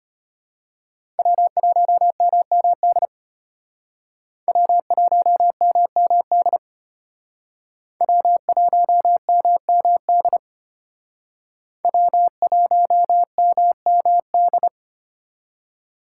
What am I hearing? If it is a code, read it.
Morse code, heard as a keyed tone, about 32 wpm, 700 Hz. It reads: W1MMD W1MMD W1MMD W1MMD